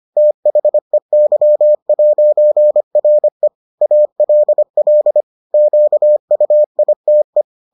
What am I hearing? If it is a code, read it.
Morse code, heard as a keyed tone, about 25 wpm, 600 Hz. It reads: THEY'RE ALL QUITE